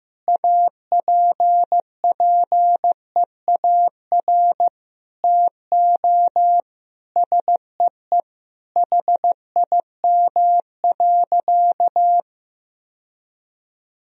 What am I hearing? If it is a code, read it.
Morse code, heard as a keyed tone, about 15 wpm, 700 Hz. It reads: APPEAR TO SEE HIM.
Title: Sentences from Top 1000 Words - Review Part 2 - 15wpm